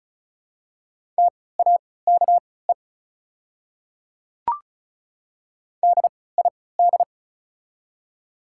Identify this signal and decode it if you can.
Morse code, heard as a keyed tone, about 36 wpm, spaced as for 20 wpm, 700 Hz. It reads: TAKE DID
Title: 3x Character Spacing - Top 200 Words - 35wpm